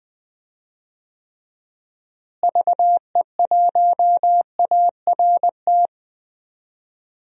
Morse code, heard as a keyed tone, about 21 wpm, 700 Hz. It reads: VE1ART
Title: DXCC-100 Call Signs - 6 Characters - Repeated 3x - 20wpm